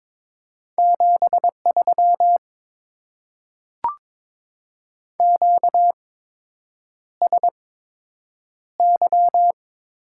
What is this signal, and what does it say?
Morse code, heard as a keyed tone, about 22 wpm, spaced as for 8 wpm, 700 Hz. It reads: 73 QSY
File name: Top 90 QSO Elements - 8x Character Spacing - 22wpm